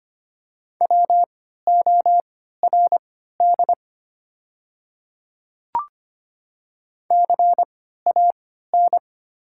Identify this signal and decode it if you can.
Morse code, heard as a keyed tone, about 25 wpm, spaced as for 14 wpm, 700 Hz. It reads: WORD CAN
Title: 3x Character Spacing - Top 100 Words - 25wpm